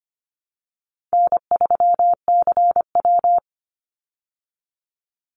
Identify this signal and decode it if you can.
Morse code, heard as a keyed tone, about 25 wpm, 700 Hz. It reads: N3CW
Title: Easy Call Signs Repeated 3x - 25wpm